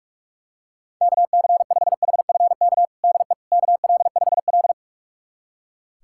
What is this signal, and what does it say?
Morse code, heard as a keyed tone, about 42 wpm, 700 Hz. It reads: KC5HFK DE KL5L
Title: USA Callsign DE Callsign 45wpm